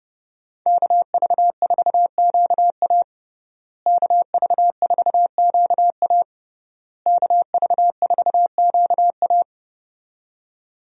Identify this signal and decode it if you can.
Morse code, heard as a keyed tone, about 32 wpm, 700 Hz. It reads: KV4QA KV4QA KV4QA
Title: USA Call Signs - 5 Characters - Repeated 3x - 30wpm